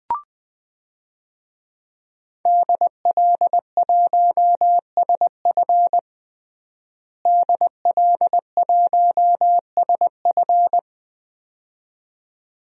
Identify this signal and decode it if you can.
Morse code, heard as a keyed tone, about 20 wpm, 700 Hz. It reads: DL1SF DL1SF
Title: DXCC-100 Call Signs - 5 Characters - Repeated 2x - 20wpm